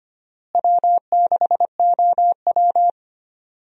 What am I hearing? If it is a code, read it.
Morse code, heard as a keyed tone, about 25 wpm, 700 Hz. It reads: W6OW
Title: USA Call Signs - 4 Characters - Repeated 3x - 25wpm